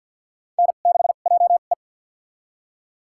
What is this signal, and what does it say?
Morse code, heard as a keyed tone, about 52 wpm, spaced as for 36 wpm, 700 Hz. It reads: N6JE
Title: USA Call Signs - 2x Character Spacing - 50wpm